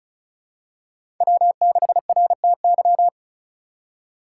Morse code, heard as a keyed tone, about 36 wpm, 700 Hz. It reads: W6RTY
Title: USA Call Signs - 5 Characters - 4x Character Spacing - 35wpm